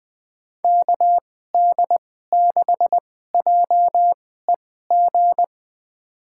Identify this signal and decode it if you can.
Morse code, heard as a keyed tone, about 20 wpm, spaced as for 14 wpm, 700 Hz. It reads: KD6JEG
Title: USA Call Signs - 2x Character Spacing - 20wpm